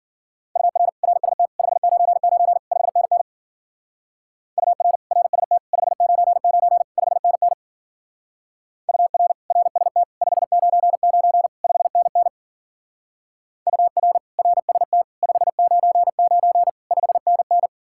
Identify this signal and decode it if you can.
Morse code, heard as a keyed tone, about 49 wpm, 700 Hz. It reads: UR RST 599 5NN UR RST 599 5NN UR RST 599 5NN UR RST 599 5NN
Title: Speed-Racing Common QSO Phrases - 40wpm